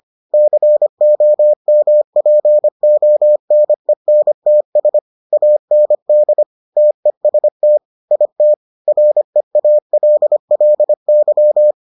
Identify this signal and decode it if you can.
Morse code, heard as a keyed tone, about 25 wpm, 600 Hz. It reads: COMPONENTS AND TEST IT REALLY